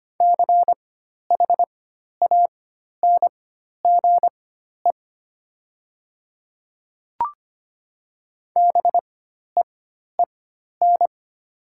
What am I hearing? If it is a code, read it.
Morse code, heard as a keyed tone, about 25 wpm, spaced as for 12 wpm, 700 Hz. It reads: CHANGE BEEN
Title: 4x Character Spacing - Top 500 Words - 25wpm